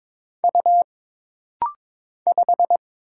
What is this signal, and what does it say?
Morse code, heard as a keyed tone, about 22 wpm, 700 Hz. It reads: U 5
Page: This rapid-fire timing is geared to push your ICR (Instant Character Recognition) skills.